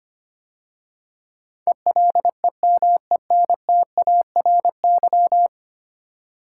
Morse code, heard as a keyed tone, about 25 wpm, 700 Hz. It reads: ELEMENTARY
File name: 10 Letter Words - First 3 Letters Given - 25wpm